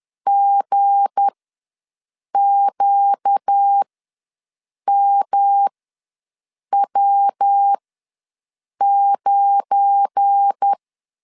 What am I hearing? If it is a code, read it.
Morse code, heard as a keyed tone, about 10 wpm, spaced as for 6 wpm, 800 Hz. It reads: GQMW9
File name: LEZIONI / 8 YJ45 / 8 YJ45 ws 10wpm 3843